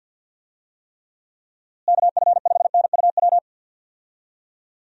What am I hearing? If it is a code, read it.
Morse code, heard as a keyed tone, about 50 wpm, 700 Hz. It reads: KF5NUW
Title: USA Call Signs - 6 Characters - Repeated 2x - 50wpm